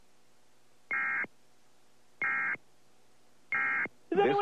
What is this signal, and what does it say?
Emergency Alert System (SAME) tones, heard as harsh buzzing bursts, with speech. It is heard from 0.9 to 3.9 s.
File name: test of the emergency broadcasting system